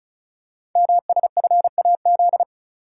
Morse code, heard as a keyed tone, about 36 wpm, 690 Hz. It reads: MSFAZ
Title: Code Groups - 5 Letters - 35wpm